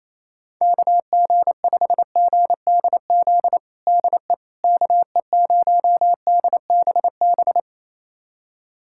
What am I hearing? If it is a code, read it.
Morse code, heard as a keyed tone, about 28 wpm, 700 Hz. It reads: KG5GDZ DE KE0DBB
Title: USA Callsign DE Callsign - 6 Characters - 28wpm